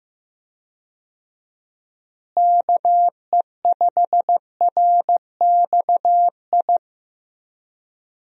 Morse code, heard as a keyed tone, about 15 wpm, 700 Hz. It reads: KE5RXI
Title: USA Call Signs - 6 Characters - Repeated 3x - 15wpm